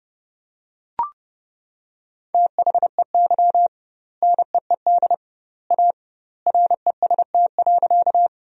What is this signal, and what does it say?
Morse code, heard as a keyed tone, about 29 wpm, 700 Hz. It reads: THEY NEED A REST.
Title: 2x Word-Spacing - Sentences from Top 500 Words - 30wpm